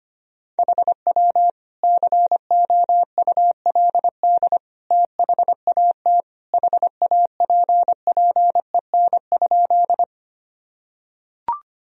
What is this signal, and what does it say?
Morse code, heard as a keyed tone, about 25 wpm, 700 Hz. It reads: HW COULD THAT HAPPEN?
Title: Sentences from Top 400 Words and Abbreviations 25wpm